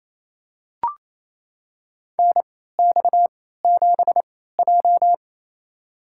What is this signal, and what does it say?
Morse code, heard as a keyed tone, about 28 wpm, spaced as for 16 wpm, 700 Hz. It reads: NX7J
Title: USA Call Signs - 4 Characters - 3x Character Spacing - 28wpm